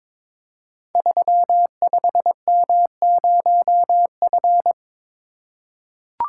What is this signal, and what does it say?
Morse code, heard as a keyed tone, about 22 wpm, 700 Hz. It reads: 35M0F
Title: Code Groups - 5 Letter:Numbers - 22wpm